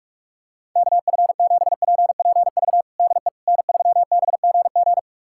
Morse code, heard as a keyed tone, about 46 wpm, 700 Hz. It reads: KF7PPV DE N3BGG
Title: USA Callsign DE Callsign 45wpm